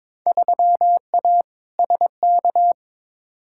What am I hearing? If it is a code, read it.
Morse code, heard as a keyed tone, about 22 wpm, 700 Hz. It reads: 3A SK
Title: Field Day Exchanges Encore - 22wpm